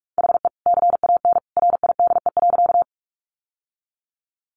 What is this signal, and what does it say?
Morse code, heard as a keyed tone, about 44 wpm, 700 Hz. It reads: HE CAN RIDE.
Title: Sentences from Top 1200 Words 45wpm